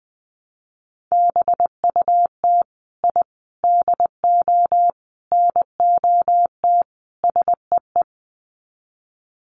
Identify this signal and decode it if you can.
Morse code, heard as a keyed tone, about 20 wpm, 700 Hz. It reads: BUT I DO NOT SEE
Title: Sets of 5 Words Encore 20wpm